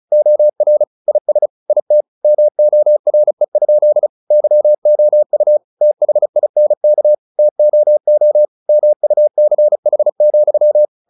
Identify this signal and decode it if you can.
Morse code, heard as a keyed tone, about 34 wpm, 600 Hz. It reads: OR IS IT MORE? YOU THINK TOO MUCH,
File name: the old man and the sea 2 35wpm